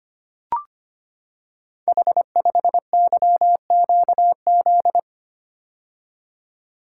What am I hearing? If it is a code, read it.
Morse code, heard as a keyed tone, about 25 wpm, 700 Hz. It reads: H5YQZ
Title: Code Groups - 5 Letter:Numbers - 25wpm